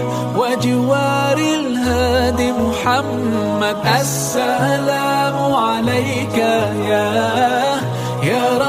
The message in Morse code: none